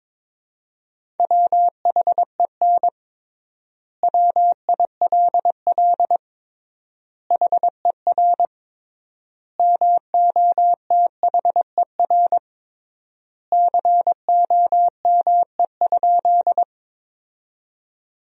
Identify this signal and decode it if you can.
Morse code, heard as a keyed tone, about 22 wpm, 700 Hz. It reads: WHEN WILL HER MOTHER COME?
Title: 3x Word-Spacing - Sentences from Top 200 Words - 22wpm